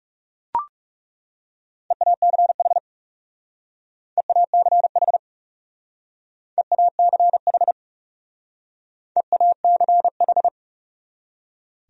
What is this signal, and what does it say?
Morse code, heard as a keyed tone, about 36 wpm, 700 Hz. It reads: EACH EACH EACH EACH
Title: Speed-Racing Top 100 Words 30wpm